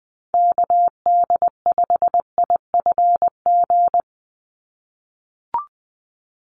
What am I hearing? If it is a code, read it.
Morse code, heard as a keyed tone, about 20 wpm, 700 Hz. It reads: KD5IFG
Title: USA Call Signs 20wpm